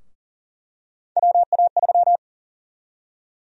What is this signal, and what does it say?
Morse code, heard as a keyed tone, about 40 wpm, 700 Hz. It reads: WA3
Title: USA Call Sign Prefixes - 40wpm